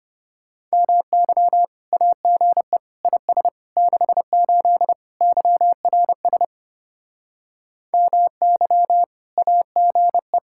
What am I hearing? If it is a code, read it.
Morse code, heard as a keyed tone, about 33 wpm, 700 Hz. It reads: MY AGE IS 68 YRS MY AGE